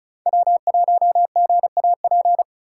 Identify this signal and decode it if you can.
Morse code, heard as a keyed tone, about 35 wpm, 690 Hz. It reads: W1GAP
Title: USA Call Signs - 5 Characters - 2x Character Spacing - 35wpm